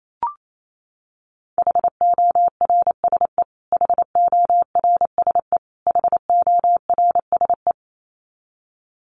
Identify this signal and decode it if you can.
Morse code, heard as a keyed tone, about 28 wpm, 700 Hz. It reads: HORSE HORSE HORSE
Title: Top 500 Words Repeated 28wpm